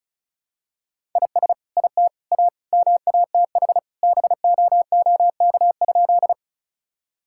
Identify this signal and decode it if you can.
Morse code, heard as a keyed tone, about 36 wpm, 700 Hz. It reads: IS IT A MATH BOOK?
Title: Sentences from Top 1000 Words - Review Part 6 - 35wpm